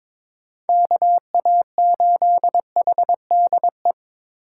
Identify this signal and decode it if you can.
Morse code, heard as a keyed tone, about 22 wpm, 700 Hz. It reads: KA8HDE
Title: K1USN SST QSO Practice - 22wpm